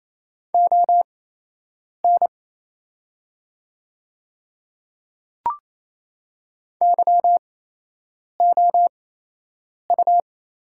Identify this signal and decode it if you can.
Morse code, heard as a keyed tone, about 28 wpm, spaced as for 8 wpm, 700 Hz. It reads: ON YOU
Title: 8x Character Spacing - Top 100 Words - 28wpm